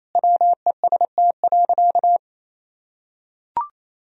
Morse code, heard as a keyed tone, about 28 wpm, 700 Hz. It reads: WEST.
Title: Sentences from Top 1000 Words - Review Part 2 - 28wpm